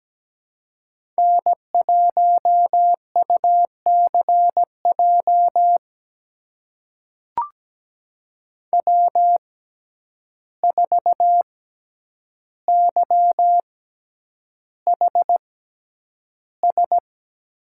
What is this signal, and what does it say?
Morse code, heard as a keyed tone, about 17 wpm, spaced as for 9 wpm, 700 Hz. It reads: N1UCJ W4YHS
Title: USA Call Signs - 5 Characters - 6x Character Spacing - 17wpm